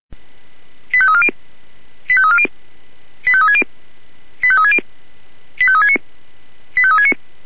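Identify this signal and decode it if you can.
Five-tone selective call (ZVEI), heard as quick runs of five tones, tones of about 70 ms. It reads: 85340 85340 85360 85360 85369 85369
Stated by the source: ZVEI five-tone selective call received by SDR.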